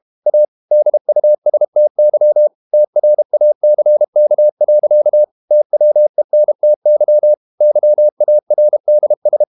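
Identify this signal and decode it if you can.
Morse code, heard as a keyed tone, about 32 wpm, 600 Hz. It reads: A DUSTY TRACK. TWENTY YARDS
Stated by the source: Morse code practice at 32 wpm from CW med Gnister, reading Ken Follett's novel Never in English.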